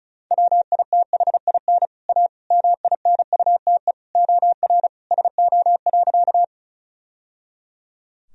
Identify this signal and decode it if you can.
Morse code, heard as a keyed tone, about 36 wpm, 700 Hz. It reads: WITHIN A MINUTE OR SO.
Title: Sentences from Top 1000 Words - Review Part 4 - 35wpm